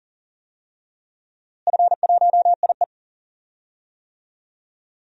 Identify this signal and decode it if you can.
Morse code, heard as a keyed tone, about 40 wpm, 700 Hz. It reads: F1IE
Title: DXCC-100 Call Signs - 4 Characters - Repeated 3x - 40wpm